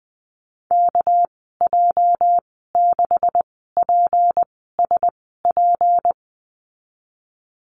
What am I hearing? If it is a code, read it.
Morse code, heard as a keyed tone, about 20 wpm, spaced as for 15 wpm, 700 Hz. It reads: KJ6PSP
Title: 245 Lesson 39g - Letter z - Call Signs 20wpm